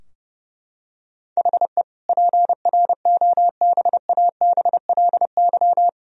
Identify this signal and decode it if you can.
Morse code, heard as a keyed tone, about 32 wpm, 700 Hz. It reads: HE PROBABLY
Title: Sentences from Top 1000 Words - Review Part 4 - 30wpm